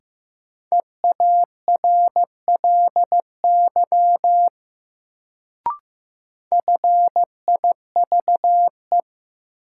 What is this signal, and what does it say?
Morse code, heard as a keyed tone, about 15 wpm, 700 Hz. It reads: EARLY FIVE
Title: Sentences from Top 400 Words - New Word Drill - 15wpm